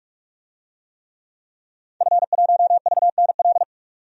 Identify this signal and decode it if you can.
Morse code, heard as a keyed tone, about 48 wpm, 700 Hz. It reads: F1VNL
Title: DXCC-100 Call Signs - 5 Characters - 45wpm